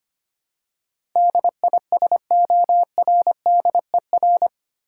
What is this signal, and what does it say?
Morse code, heard as a keyed tone, about 25 wpm, 700 Hz. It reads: DISORDER
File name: 8 Letter Words - First 3 Letters Given - 25wpm